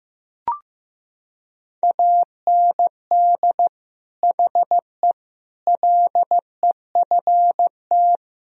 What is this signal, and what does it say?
Morse code, heard as a keyed tone, about 16 wpm, 700 Hz. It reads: AND HE LEFT
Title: Sentences from Top 1000 Words - Review Part 3 - 15wpm